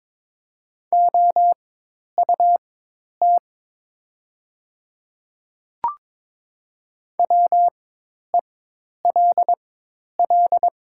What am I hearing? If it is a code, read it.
Morse code, heard as a keyed tone, about 22 wpm, spaced as for 10 wpm, 700 Hz. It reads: OUT WELL
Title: 4x Character Spacing - Top 300 Words - 22wpm